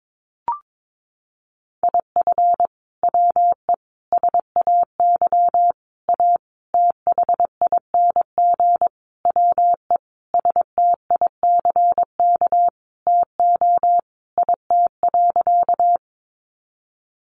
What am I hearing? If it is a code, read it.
Morse code, heard as a keyed tone, about 22 wpm, 700 Hz. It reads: IF WE SAY A THING WE STICK TO IT.
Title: Sentences from Top 1200 Words 22wpm